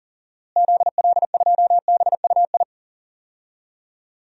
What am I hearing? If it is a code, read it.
Morse code, heard as a keyed tone, about 39 wpm, 700 Hz. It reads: ZL2BUI
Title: DXCC-100 Call Signs - 6 Characters - Repeated 2x - 40wpm